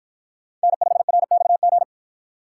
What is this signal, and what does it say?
Morse code, heard as a keyed tone, about 53 wpm, 700 Hz. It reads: N5RXG